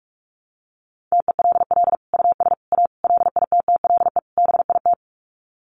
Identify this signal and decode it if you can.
Morse code, heard as a keyed tone, about 44 wpm, 700 Hz. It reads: TELL US A LITTLE BIT
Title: Sets of 5 Words Encore2 45wpm